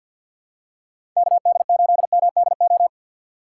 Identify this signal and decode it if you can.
Morse code, heard as a keyed tone, about 50 wpm, 700 Hz. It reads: KD8MDO
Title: USA Call Signs - 6 Characters - 6x Character Spacing - 50wpm